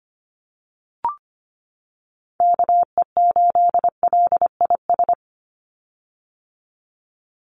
Morse code, heard as a keyed tone, about 25 wpm, 690 Hz. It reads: KE8LIS
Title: USA Call Signs 25wpm